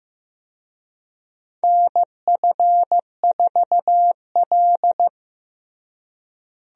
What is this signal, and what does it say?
Morse code, heard as a keyed tone, about 15 wpm, 700 Hz. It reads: NF4L
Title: USA Call Signs - 4 Characters - Repeated 3x - 15wpm